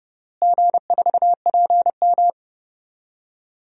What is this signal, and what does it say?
Morse code, heard as a keyed tone, about 28 wpm, 700 Hz. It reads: G4PM
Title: DXCC-100 Call Signs - 4 Characters - Repeated 3x - 30wpm